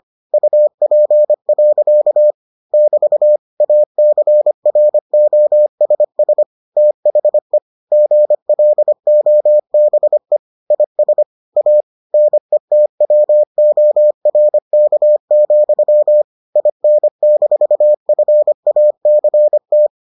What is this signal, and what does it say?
Morse code, heard as a keyed tone, about 25 wpm, 600 Hz. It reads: UP. <BT> ACROSS THE GLOBE IS A NETWORK, IN-FACT